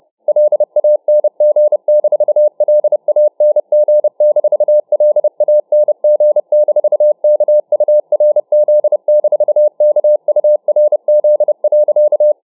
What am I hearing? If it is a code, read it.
Morse code, heard as a keyed tone, about 31 wpm, 600 Hz. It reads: LANG-LANG-LANG-KURZ-KURZ.